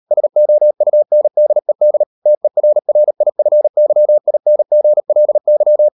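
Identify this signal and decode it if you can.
Morse code, heard as a keyed tone, about 38 wpm, 600 Hz. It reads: SOUNDED TERRIFYINGLY
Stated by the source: Morse code practice at 38 WPM, sending Ken Follett's novel Never in English, from CW med Gnister.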